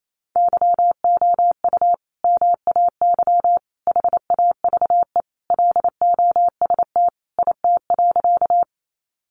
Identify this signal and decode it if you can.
Morse code, heard as a keyed tone, about 28 wpm, 700 Hz. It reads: YOU MAY HAVE LOST IT.
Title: Sentences from Top 1400 Words 28wpm